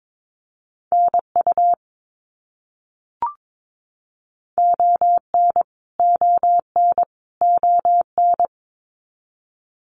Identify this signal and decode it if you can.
Morse code, heard as a keyed tone, about 22 wpm, 700 Hz. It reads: NU ON ON ON
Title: Canadian Provinces and Territories - Repeated 3x - 22wpm